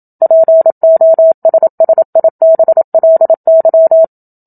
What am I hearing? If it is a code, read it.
Morse code, heard as a keyed tone, about 27 wpm, 650 Hz. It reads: POSSIBLY
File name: Book Two / 01-Chapter 1 - 27 WPM